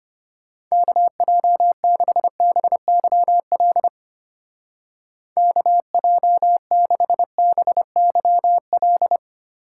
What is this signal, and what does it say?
Morse code, heard as a keyed tone, about 33 wpm, 700 Hz. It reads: KJ6BYL KJ6BYL